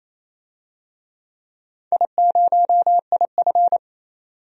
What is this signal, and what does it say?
Morse code, heard as a keyed tone, about 28 wpm, 700 Hz. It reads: I0IF
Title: DXCC-100 Call Signs - 4 Characters - 28wpm